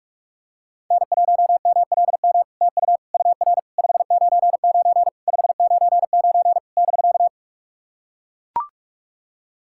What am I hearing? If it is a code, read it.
Morse code, heard as a keyed tone, about 44 wpm, 690 Hz. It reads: N1MLM TU UR 599 599 <BK>
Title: SOTA QSO Practice 45wpm